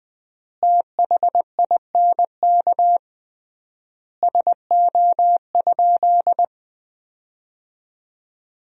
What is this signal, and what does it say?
Morse code, heard as a keyed tone, about 20 wpm, 700 Hz. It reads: THINK SO?
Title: 3x Word-Spacing - Sentences from Top 300 Words - 20wpm